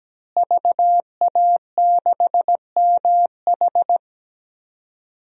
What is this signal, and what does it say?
Morse code, heard as a keyed tone, about 17 wpm, 700 Hz. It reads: VA6MH
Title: DXCC-100 Call Signs - 5 Characters - Repeated 3x - 17wpm